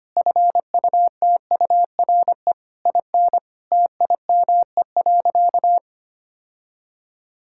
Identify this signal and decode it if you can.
Morse code, heard as a keyed tone, about 25 wpm, 700 Hz. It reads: FUTURE IN TIME.
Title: Sentences from Top 1000 Words - Review Part 6 - 25wpm